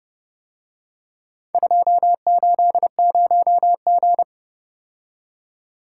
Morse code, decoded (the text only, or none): 280G